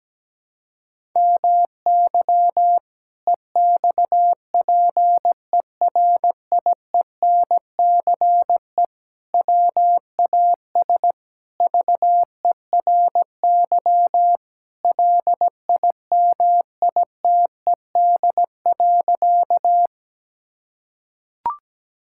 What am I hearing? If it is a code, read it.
Morse code, heard as a keyed tone, about 17 wpm, 700 Hz. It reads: MY EXPERIENCE WAS VERY LIMITED.
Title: Sentences from Top 1000 Words - Review Part 6 - 17wpm